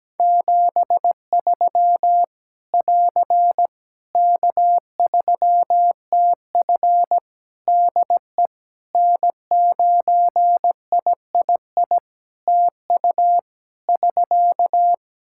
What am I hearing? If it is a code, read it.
Morse code, heard as a keyed tone, about 17 wpm, 700 Hz. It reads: 73 <AR> K3TF DE N9III TU <SK>
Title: Practice QSO - Exchange 4 - Ending - 17wpm